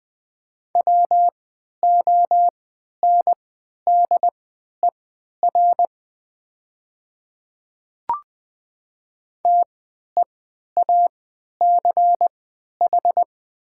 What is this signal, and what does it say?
Morse code, heard as a keyed tone, about 20 wpm, spaced as for 11 wpm, 700 Hz. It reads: WONDER TEACH
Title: 3x Character Spacing - Top 500 Words - 20wpm